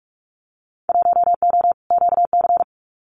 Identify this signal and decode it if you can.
Morse code, heard as a keyed tone, about 44 wpm, 700 Hz. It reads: 1O QC
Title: Winter Field Day Exchanges 45wpm